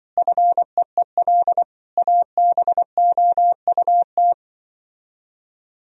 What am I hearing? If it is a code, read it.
Morse code, heard as a keyed tone, about 24 wpm, 700 Hz. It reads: FEEL ABOUT
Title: Speed-Racing - Sets of 5 Words 20wpm